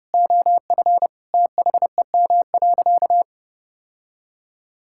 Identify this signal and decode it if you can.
Morse code, heard as a keyed tone, about 30 wpm, 700 Hz. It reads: OF THEM.
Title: Sentences from Top 1000 Words - Review Part 5 - 30wpm